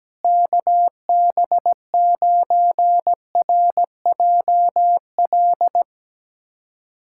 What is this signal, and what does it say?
Morse code, heard as a keyed tone, about 17 wpm, 700 Hz. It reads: KB9RJL
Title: USA Callsign DE Callsign - 6 Characters - 17wpm